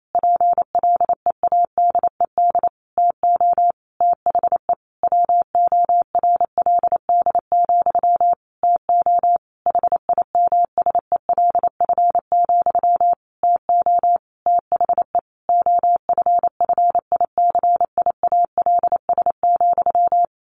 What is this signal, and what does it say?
Morse code, heard as a keyed tone, about 28 wpm, 700 Hz. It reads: PLEADED TO THE WORLD, TO HIMSELF, TO THE OFFICIALS,
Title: A Sound Of Thunder 28wpm